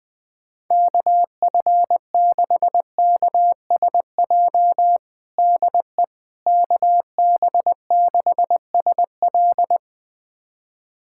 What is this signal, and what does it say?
Morse code, heard as a keyed tone, about 21 wpm, 700 Hz. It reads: KF6KSJ DE KB6SL
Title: USA Callsign DE Callsign 20wpm